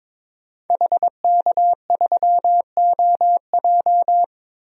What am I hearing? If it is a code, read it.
Morse code, heard as a keyed tone, about 22 wpm, 700 Hz. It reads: HK3OJ
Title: DXCC-100 Call Signs - 5 Characters - 22wpm